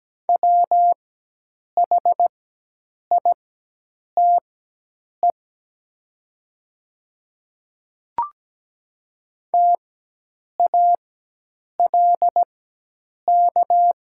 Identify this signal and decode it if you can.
Morse code, heard as a keyed tone, about 17 wpm, spaced as for 8 wpm, 700 Hz. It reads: WHITE TALK
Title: 4x Character Spacing - Top 500 Words - 17wpm